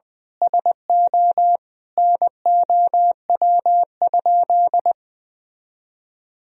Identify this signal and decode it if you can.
Morse code, heard as a keyed tone, about 20 wpm, 700 Hz. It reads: SO NOW?
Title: Sentences from Top 200 Words and Abbreviations 20wpm